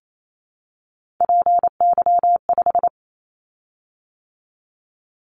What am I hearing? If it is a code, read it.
Morse code, heard as a keyed tone, about 28 wpm, 700 Hz. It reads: PY5
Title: Callsign Prefixes for Top 100 DXCC - 28wpm